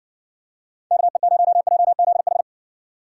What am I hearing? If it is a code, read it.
Morse code, heard as a keyed tone, about 60 wpm, 700 Hz. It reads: KE0JZH